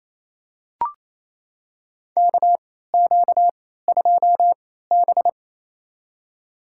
Morse code, heard as a keyed tone, about 28 wpm, spaced as for 16 wpm, 700 Hz. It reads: KQ2B
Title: USA Call Signs - 4 Characters - 3x Character Spacing - 28wpm